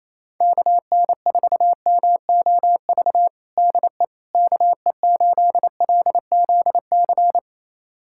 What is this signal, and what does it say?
Morse code, heard as a keyed tone, about 27 wpm, 700 Hz. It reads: KN4MOV DE KE8LZC
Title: USA Callsign DE Callsign - 6 Characters - 28wpm